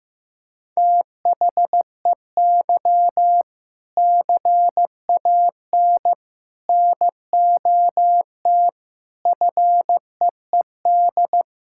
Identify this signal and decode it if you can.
Morse code, heard as a keyed tone, about 15 wpm, 700 Hz. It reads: THEY CAN NOT FEED
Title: Sentences from Top 1600 Words 15wpm